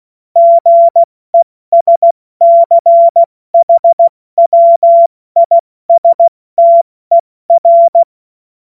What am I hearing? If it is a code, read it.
Morse code, heard as a keyed tone, about 16 wpm, 680 Hz. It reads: GESCHWISTER
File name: Morsekurs / Witze / Witz 17. Januar 2023 70 F80